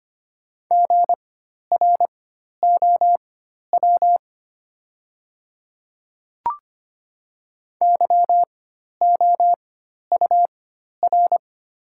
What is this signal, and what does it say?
Morse code, heard as a keyed tone, about 25 wpm, spaced as for 12 wpm, 700 Hz. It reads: GROW YOUR
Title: 4x Character Spacing - Top 500 Words - 25wpm